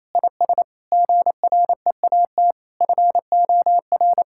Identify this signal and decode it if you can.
Morse code, heard as a keyed tone, about 28 wpm, 700 Hz. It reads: IS GREAT FOR
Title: Sentences from Top 1000 Words - Review Part 5 - 28wpm